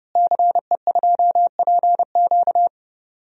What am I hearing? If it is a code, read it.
Morse code, heard as a keyed tone, about 30 wpm, 700 Hz. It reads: CE2PQ
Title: DXCC-100 Call Signs - 5 Characters - Repeated 2x - 30wpm